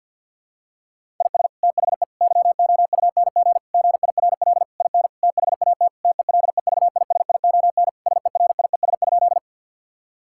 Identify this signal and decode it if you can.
Morse code, heard as a keyed tone, about 52 wpm, 700 Hz. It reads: IS THE YOUNG GIRL IN THAT TELEVISION SERIES?
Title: Sentences from Top 1000 Words - Review Part 5 - 50wpm